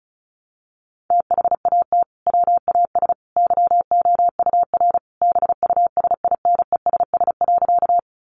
Morse code, heard as a keyed tone, about 34 wpm, 700 Hz. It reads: THAT WAS YOUR BUSINESS.
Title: Sentences from Top 600 Words 35wpm